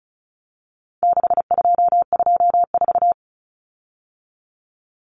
Morse code, heard as a keyed tone, about 34 wpm, 700 Hz. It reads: 6224
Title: Common Cut Numbers - 4-Digit 35wpm